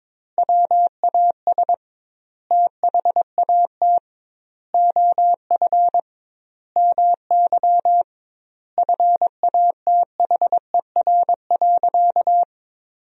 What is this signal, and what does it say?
Morse code, heard as a keyed tone, about 22 wpm, 700 Hz. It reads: WAS THAT OF MY FATHER.
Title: 2x Word-Spacing - Sentences from Top 500 Words - 22wpm